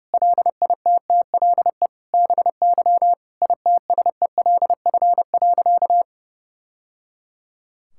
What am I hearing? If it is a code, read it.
Morse code, heard as a keyed tone, about 32 wpm, 700 Hz. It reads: LITTLE BY ITSELF.
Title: Sentences from Top 1000 Words - Review Part 5 - 30wpm